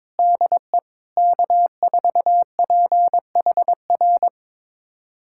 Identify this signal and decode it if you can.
Morse code, heard as a keyed tone, about 22 wpm, 700 Hz. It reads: DE K4PHR
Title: USA Callsign DE Callsign 22wpm